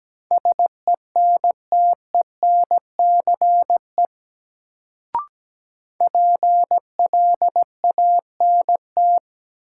Morse code, heard as a keyed tone, about 17 wpm, 700 Hz. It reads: SENTENCE PLANT
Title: Sentences from Top 1000 Words - Review Part 1 - New Word Drill - 17wpm